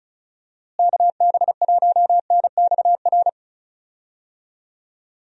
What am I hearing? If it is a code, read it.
Morse code, heard as a keyed tone, about 34 wpm, 700 Hz. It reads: KB1NXR
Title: USA Call Signs - 6 Characters - Repeated 3x - 35wpm